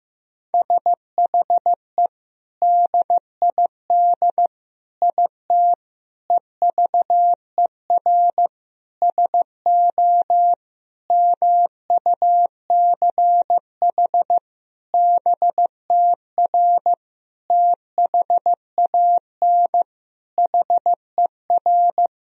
Morse code, heard as a keyed tone, about 15 wpm, 700 Hz. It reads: SHE DID IT EVER SO MUCH BTR THAN HER
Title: Sentences from Top 400 Words and Abbreviations 15wpm